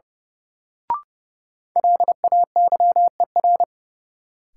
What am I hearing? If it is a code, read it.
Morse code, heard as a keyed tone, about 30 wpm, 700 Hz. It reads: LAYER
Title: Sentences from Top 1600 Words - New Word Drill Alt - 30wpm